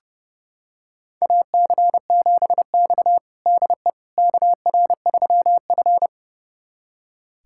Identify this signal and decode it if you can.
Morse code, heard as a keyed tone, about 32 wpm, 700 Hz. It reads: AC7X DE KR3F
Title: USA Callsign DE Callsign - 4 Characters - 30wpm